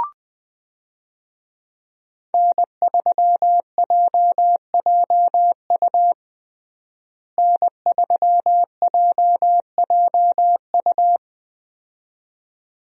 Morse code, heard as a keyed tone, about 20 wpm, 700 Hz. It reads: N3JJU N3JJU
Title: USA Call Signs - 5 Characters - Repeated 2x - 20wpm